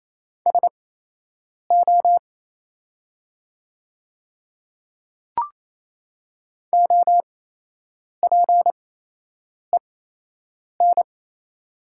Morse code, heard as a keyed tone, about 28 wpm, spaced as for 8 wpm, 700 Hz. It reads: SO OPEN
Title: 8x Character Spacing - Top 300 Words - 28wpm